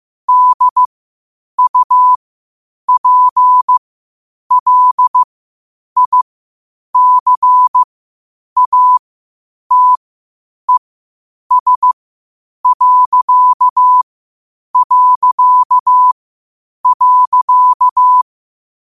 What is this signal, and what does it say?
Morse code, heard as a keyed tone, about 15 wpm, spaced as for 8 wpm, 1000 Hz. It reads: DUPLICATES...
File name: qotdfiles / 8wpm